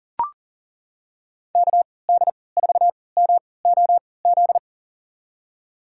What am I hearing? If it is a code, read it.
Morse code, heard as a keyed tone, about 40 wpm, spaced as for 23 wpm, 700 Hz. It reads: KD4MOZ
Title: USA Call Signs - 6 Characters - 3x Character Spacing - 40wpm